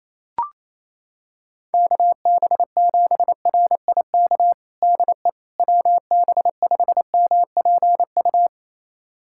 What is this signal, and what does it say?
Morse code, heard as a keyed tone, about 28 wpm, 700 Hz. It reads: KB7RIK DE WB5MPU
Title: USA Callsign DE Callsign - 6 Characters - 28wpm